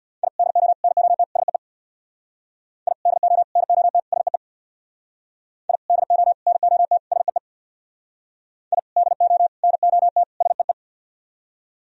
Morse code, heard as a keyed tone, about 61 wpm, 700 Hz. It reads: I DO NOT SEE I DO NOT SEE I DO NOT SEE I DO NOT SEE
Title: Speed-Racing - Sets of 4 Words 50wpm